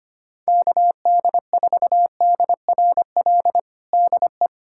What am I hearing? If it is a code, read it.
Morse code, heard as a keyed tone, about 25 wpm, 700 Hz. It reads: KD4DRL DE
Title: USA Callsign DE Callsign - 6 Characters - 25wpm